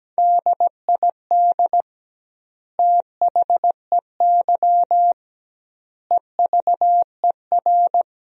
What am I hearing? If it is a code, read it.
Morse code, heard as a keyed tone, about 17 wpm, 700 Hz. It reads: DID THEY EVER
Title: 2x Word-Spacing - Sentences from Top 400 Words - 17wpm